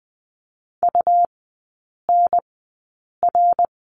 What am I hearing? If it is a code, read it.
Morse code, heard as a keyed tone, about 20 wpm, spaced as for 8 wpm, 700 Hz. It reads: UNR
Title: 100 Lesson 17e - Letter u - Three Characters 20wpm